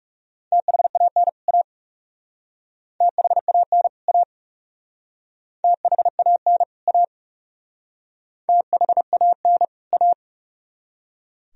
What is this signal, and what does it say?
Morse code, heard as a keyed tone, about 38 wpm, 700 Hz. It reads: THAN A THAN A THAN A THAN A